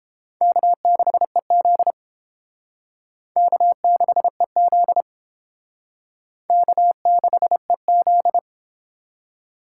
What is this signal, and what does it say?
Morse code, heard as a keyed tone, about 32 wpm, 700 Hz. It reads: K6EZ K6EZ K6EZ